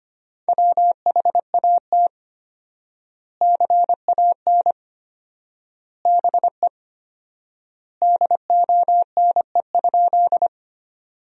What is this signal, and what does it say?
Morse code, heard as a keyed tone, about 25 wpm, 700 Hz. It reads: WHAT CAN BE DONE?
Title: 4x Word-Spacing - Sentences from Top 500 Words - 25wpm